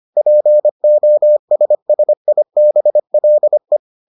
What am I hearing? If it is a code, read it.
Morse code, heard as a keyed tone, about 25 wpm, 600 Hz. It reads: POSSIBLE